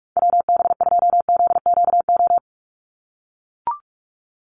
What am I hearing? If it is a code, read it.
Morse code, heard as a keyed tone, about 44 wpm, 700 Hz. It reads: WB2ZQO
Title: USA Call Signs 45wpm